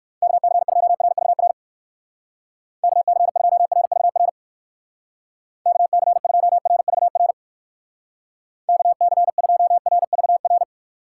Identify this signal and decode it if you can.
Morse code, heard as a keyed tone, about 55 wpm, 700 Hz. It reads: KC2RVR KC2RVR KC2RVR KC2RVR
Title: Speed Racing - USA Call Signs 45wpm